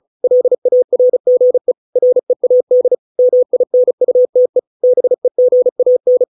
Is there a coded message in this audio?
Yes: Morse code from 0.2 to 6.2 s, heard as a keyed tone, about 34 wpm, 500 Hz. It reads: LARGE READ MINUTE BEGAN